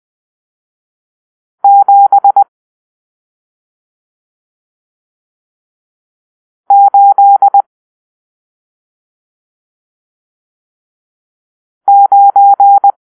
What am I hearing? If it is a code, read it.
Morse code, heard as a keyed tone, about 20 wpm, 800 Hz. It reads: 7 8 9
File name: arrl / Grp-7 Drill-1